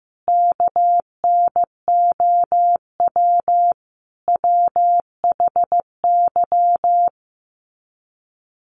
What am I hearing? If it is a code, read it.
Morse code, heard as a keyed tone, about 15 wpm, 700 Hz. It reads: KNOW WHY
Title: Sets of 5 Words 15wpm